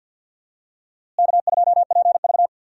Morse code, heard as a keyed tone, about 53 wpm, 700 Hz. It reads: K2PV
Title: USA Call Signs - 4 Characters - Repeated 3x - 50wpm